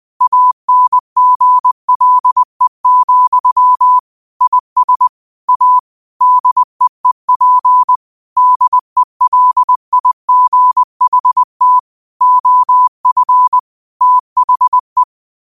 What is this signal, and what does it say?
Morse code, heard as a keyed tone, about 20 wpm, 1000 Hz. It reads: ANGLE, IS A DEEP DELIGHT OF THE
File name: qotdfiles / 20wpm